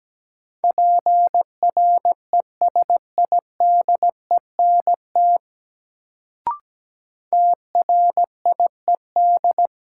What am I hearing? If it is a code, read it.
Morse code, heard as a keyed tone, about 17 wpm, 700 Hz. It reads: PRESIDENT TRIED
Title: Sets of 2 Words - New Word Drill - 17wpm